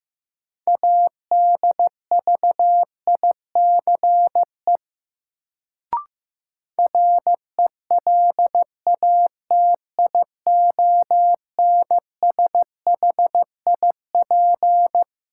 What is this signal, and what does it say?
Morse code, heard as a keyed tone, about 15 wpm, 700 Hz. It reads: ADVICE RELATIONSHIP
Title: Sentences from Top 1100 Words - New Word Drill Alt - 15wpm